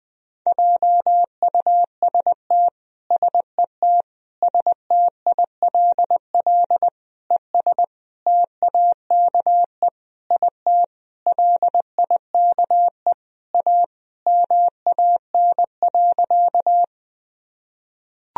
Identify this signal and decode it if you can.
Morse code, heard as a keyed tone, about 21 wpm, 700 Hz. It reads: JUST SET STILL ES TAKE IT LIKE A MAN.
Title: Sentences from Top 300 Words and Abbreviations 20wpm